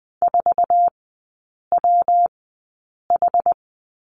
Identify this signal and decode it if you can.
Morse code, heard as a keyed tone, about 20 wpm, spaced as for 8 wpm, 700 Hz. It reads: 4WH
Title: 114 Lesson 19e - Letter w - Three Characters 20wpm